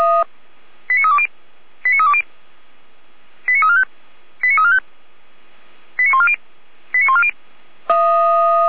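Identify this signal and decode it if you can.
Five-tone selective call (ZVEI), heard as quick runs of five tones, tones of about 70 ms. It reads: 89320 89320 89345 89345 89140 89140
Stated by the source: ZVEI five-tone call 89501, received by SDR.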